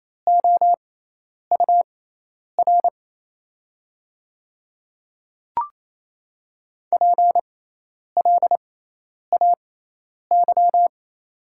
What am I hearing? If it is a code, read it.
Morse code, heard as a keyed tone, about 28 wpm, spaced as for 10 wpm, 700 Hz. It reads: OUR PLAY
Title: 6x Character Spacing - Top 300 Words - 28wpm